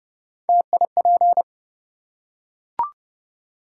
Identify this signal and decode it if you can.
Morse code, heard as a keyed tone, about 32 wpm, 700 Hz. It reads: TIP
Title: Three Letter Words - 30wpm